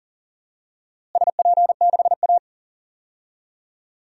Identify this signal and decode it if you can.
Morse code, heard as a keyed tone, about 40 wpm, 700 Hz. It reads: SP6A
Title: DXCC-100 Call Signs - 4 Characters - Repeated 2x - 40wpm